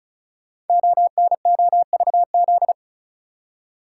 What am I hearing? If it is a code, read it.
Morse code, heard as a keyed tone, about 36 wpm, 700 Hz. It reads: ONOVZ
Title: Code Groups - 5 Letters - 35wpm